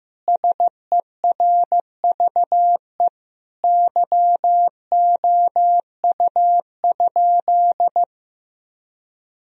Morse code, heard as a keyed tone, about 15 wpm, 700 Hz. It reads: SERVE YOU?
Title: Sentences from Top 1000 Words - Review Part 2 - 15wpm